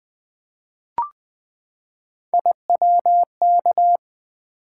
Morse code, heard as a keyed tone, about 20 wpm, 700 Hz. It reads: IWK
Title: Three Letter-Number Combinations - 20wpm